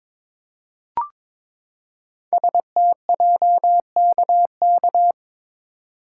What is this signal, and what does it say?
Morse code, heard as a keyed tone, about 22 wpm, 700 Hz. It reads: STJKK
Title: Code Groups - 5 Letters - 22wpm